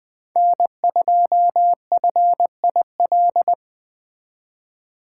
Morse code, heard as a keyed tone, about 20 wpm, 700 Hz. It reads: N2FIL